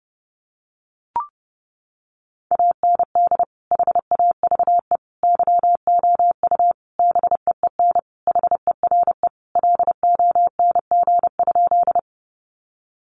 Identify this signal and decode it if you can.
Morse code, heard as a keyed tone, about 30 wpm, 700 Hz. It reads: AND HAVE YOU BEEN HERE LONG?
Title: Sentences from Top 200 Words 30wpm